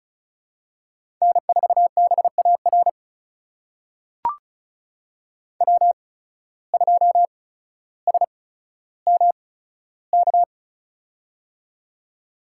Morse code, heard as a keyed tone, about 36 wpm, spaced as for 15 wpm, 700 Hz. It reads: N4BAR W2SMK
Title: USA Call Signs - 5 Characters - 8x Character Spacing - 35wpm